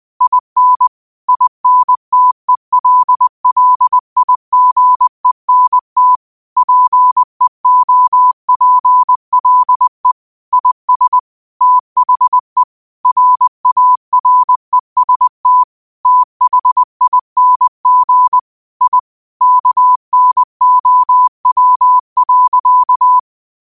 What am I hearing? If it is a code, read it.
Morse code, heard as a keyed tone, about 20 wpm, 1000 Hz. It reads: IN INTELLIGENT PEOPLE IS THE RAREST THING I KNOW.
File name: qotdfiles / 20wpm